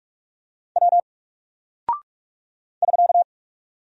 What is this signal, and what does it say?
Morse code, heard as a keyed tone, about 44 wpm, 700 Hz. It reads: W <SK>
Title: Single Character (Letter:Number:Punctuation:Prosign) 45wpm